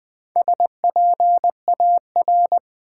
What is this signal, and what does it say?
Morse code, heard as a keyed tone, about 20 wpm, 700 Hz. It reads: SPAR
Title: Build A Word - 1 Letter At A Time - 5-Letter Words - 20wpm